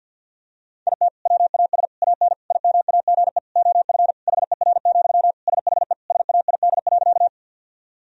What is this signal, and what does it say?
Morse code, heard as a keyed tone, about 50 wpm, 700 Hz. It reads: IT WAS AN IMAGE OF HER, SHE SAID.
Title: Sentences from Top 1000 Words - Review Part 4 - 50wpm